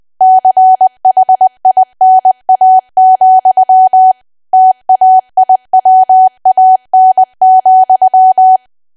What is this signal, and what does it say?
Morse code, heard as a keyed tone, about 20 wpm, 750 Hz. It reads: CHINA, TAIWAN,
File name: cwa / W1AW-Level3 / 20WPM